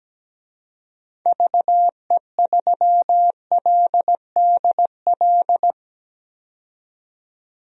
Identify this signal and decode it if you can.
Morse code, heard as a keyed tone, about 17 wpm, 700 Hz. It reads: VE3LDL
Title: DXCC-100 Call Signs - 6 Characters - 17wpm